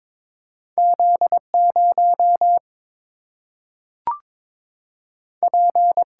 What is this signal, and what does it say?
Morse code, heard as a keyed tone, about 22 wpm, 700 Hz. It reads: Z 0 P
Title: Single-Letter Single-Number 22wpm